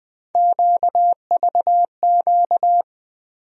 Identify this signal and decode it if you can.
Morse code, heard as a keyed tone, about 20 wpm, 700 Hz. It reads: QVQ
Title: USA Call Sign Postfixes - 3 Letters 20wpm